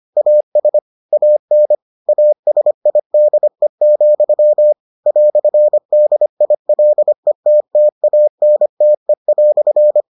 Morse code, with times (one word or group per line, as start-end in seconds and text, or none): 0.2-0.8 AS
1.1-1.7 AN
2.1-4.7 ASIDE,
5.1-10.0 "DILETTANTE"